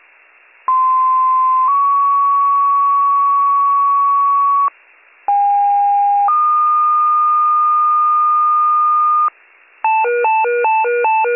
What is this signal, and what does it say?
Two-tone page, heard as a one-second tone, then a three-second tone, about 1040 then 1120 Hz, about 810 then 1220 Hz. It is heard from 0.7 to 9.3 s.